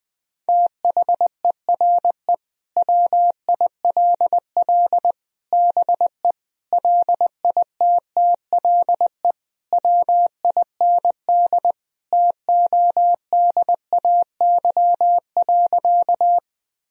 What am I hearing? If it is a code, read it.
Morse code, heard as a keyed tone, about 21 wpm, 700 Hz. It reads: THERE WILL BE LITTLE WIND TODAY.
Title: Sentences from Top 1000 Words - Review Part 2 - 20wpm